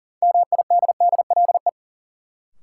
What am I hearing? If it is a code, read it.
Morse code, heard as a keyed tone, about 40 wpm, 700 Hz. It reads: MIDDLE
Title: Sets of 2 Words - New Word Drill - 40wpm